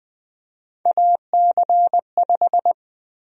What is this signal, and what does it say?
Morse code, heard as a keyed tone, about 20 wpm, 700 Hz. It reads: AC5
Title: USA Call Sign Prefixes - 20wpm